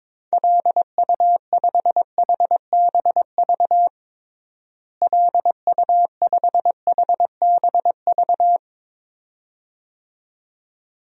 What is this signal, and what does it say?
Morse code, heard as a keyed tone, about 22 wpm, 700 Hz. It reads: LU5HBV LU5HBV
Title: DXCC-100 Call Signs - 6 Characters - Repeated 2x - 22wpm